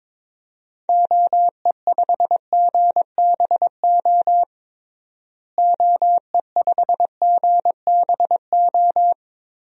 Morse code, heard as a keyed tone, about 22 wpm, 700 Hz. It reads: OE5GBO OE5GBO
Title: DXCC-100 Call Signs - 6 Characters - Repeated 2x - 22wpm